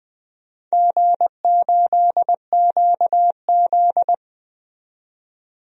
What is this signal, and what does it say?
Morse code, heard as a keyed tone, about 20 wpm, 700 Hz. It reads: G8QZ
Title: DXCC-100 Call Signs - 4 Characters - 20wpm